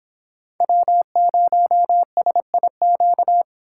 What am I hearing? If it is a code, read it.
Morse code, heard as a keyed tone, about 26 wpm, 700 Hz. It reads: W0SIQ